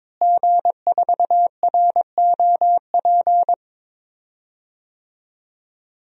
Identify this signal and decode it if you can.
Morse code, heard as a keyed tone, about 22 wpm, 700 Hz. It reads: G4ROP
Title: DXCC-100 Call Signs - 5 Characters - Repeated 2x - 22wpm